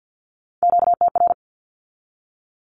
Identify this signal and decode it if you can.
Morse code, heard as a keyed tone, about 50 wpm, 700 Hz. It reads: QTR
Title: Q-Codes with Spoken Meaning - 50wpm